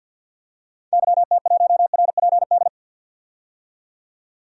Morse code, heard as a keyed tone, about 48 wpm, 700 Hz. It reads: YT1RPD
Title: DXCC-100 Call Signs - 6 Characters - 50wpm